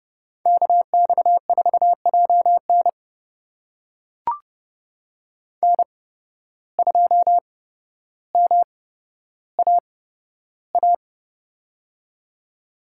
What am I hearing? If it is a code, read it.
Morse code, heard as a keyed tone, about 30 wpm, spaced as for 13 wpm, 700 Hz. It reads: KX4JN N2MAA
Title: USA Call Signs - 5 Characters - 8x Character Spacing - 30wpm